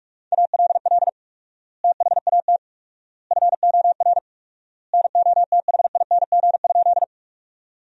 Morse code, heard as a keyed tone, about 41 wpm, 700 Hz. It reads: ALL THAT FOR NOTHING?
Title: 4x Word-Spacing - Sentences from Top 500 Words - 45wpm